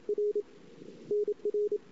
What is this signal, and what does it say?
Morse code, heard as a keyed tone, about 32 wpm, 400 Hz. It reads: R NR
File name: CW / 2008-ss-cw-k5tr-op-n5rz -23:00